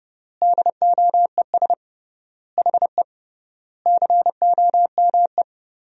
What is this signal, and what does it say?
Morse code, heard as a keyed tone, about 32 wpm, 700 Hz. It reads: DOES HE COME
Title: 3x Word-Spacing - Sentences from Top 300 Words - 30wpm